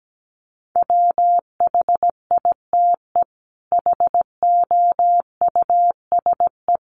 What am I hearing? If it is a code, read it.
Morse code, heard as a keyed tone, about 17 wpm, 700 Hz. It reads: WHITE HOUSE
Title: Sets of 5 Words Encore 17wpm